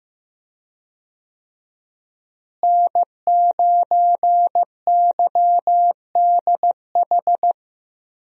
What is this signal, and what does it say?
Morse code, heard as a keyed tone, about 15 wpm, 700 Hz. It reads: N9YDH
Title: USA Call Signs - 5 Characters - Repeated 3x - 15wpm